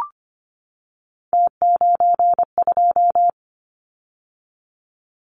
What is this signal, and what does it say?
Morse code, heard as a keyed tone, about 25 wpm, 700 Hz. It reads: T92
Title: Callsign Prefixes for Top 100 DXCC - 25wpm